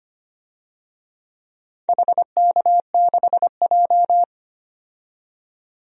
Morse code, heard as a keyed tone, about 25 wpm, 700 Hz. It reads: HK6J
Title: DXCC-100 Call Signs - 4 Characters - 25wpm